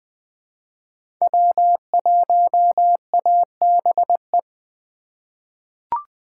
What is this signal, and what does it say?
Morse code, heard as a keyed tone, about 20 wpm, 700 Hz. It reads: W1ABE
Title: USA Call Signs - 5 Characters - 4x Character Spacing - 20wpm